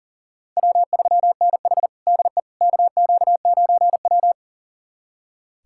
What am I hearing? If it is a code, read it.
Morse code, heard as a keyed tone, about 40 wpm, 700 Hz. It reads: W3NH DE KQ9W
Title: USA Callsign DE Callsign - 4 Characters - 40wpm